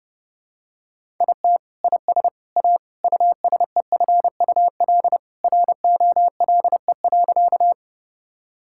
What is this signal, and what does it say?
Morse code, heard as a keyed tone, about 30 wpm, 700 Hz. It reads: IT IS A USEFUL ROLE.
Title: Sentences from Top 1000 Words - Review Part 6 - 30wpm